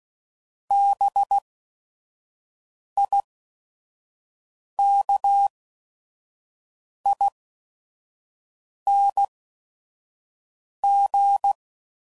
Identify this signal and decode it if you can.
Morse code, heard as a keyed tone, about 16 wpm, spaced as for 5 wpm, 800 Hz. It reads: BIKING